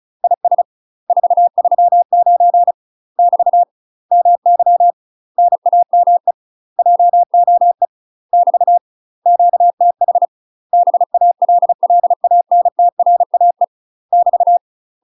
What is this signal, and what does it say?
Morse code, heard as a keyed tone, about 36 wpm, 700 Hz. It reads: IS 439 <BT> MY NAME JOE <BT> QTH BALLANTRAE <BT>
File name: UK MASTER QSOs TRAINING / 35WPM / UK MASTER QSOs 0032